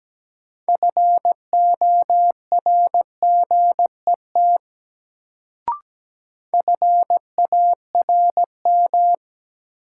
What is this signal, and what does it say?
Morse code, heard as a keyed tone, about 17 wpm, 700 Hz. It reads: FORGET FARM
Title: Sentences from Top 1200 Words - New Word Drill Alt - 17wpm